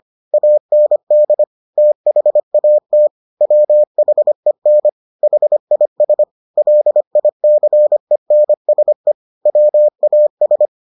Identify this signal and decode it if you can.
Morse code, heard as a keyed tone, about 25 wpm, 600 Hz. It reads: AND THAT WHEN HIS LICENSE WAS